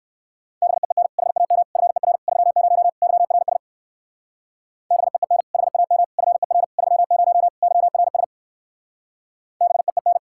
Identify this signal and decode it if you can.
Morse code, heard as a keyed tone, about 59 wpm, 700 Hz. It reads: BEEN HAM FER 30 YRS BEEN HAM FER 30 YRS BEEN